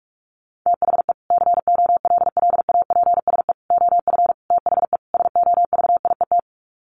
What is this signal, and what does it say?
Morse code, heard as a keyed tone, about 45 wpm, 700 Hz. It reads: THE COLLAPSE OF THE SOVIET
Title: Sets of 5 Words Encore 45wpm